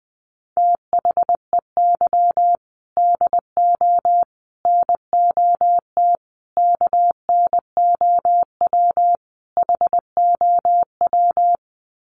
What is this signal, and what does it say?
Morse code, heard as a keyed tone, about 20 wpm, 700 Hz. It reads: THEY DO NOT KNOW HOW
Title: Sets of 5 Words Encore2 20wpm